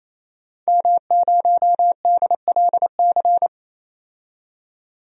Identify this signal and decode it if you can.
Morse code, heard as a keyed tone, about 28 wpm, 700 Hz. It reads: M0DLC
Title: DXCC-100 Call Signs - 5 Characters - 28wpm